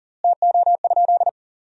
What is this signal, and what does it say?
Morse code, heard as a keyed tone, about 37 wpm, 700 Hz. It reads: TO?
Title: 4x Word-Spacing - Sentences from Top 100 Words - 40wpm